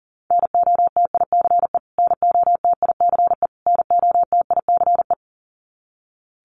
Morse code, heard as a keyed tone, about 40 wpm, 700 Hz. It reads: NOTICE NOTICE NOTICE
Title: Top 500 Words Repeated 40wpm